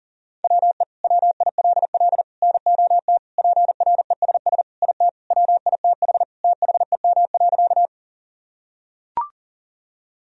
Morse code, heard as a keyed tone, about 40 wpm, 700 Hz. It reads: WE WILL NOT PRESS IT WITH THEM.
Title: Sentences from Top 1000 Words - Review Part 1 - 40wpm